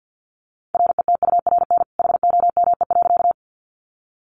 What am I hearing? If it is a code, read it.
Morse code, heard as a keyed tone, about 50 wpm, 700 Hz. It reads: RETURN HOME.
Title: Sentences from Top 800 Words 50wpm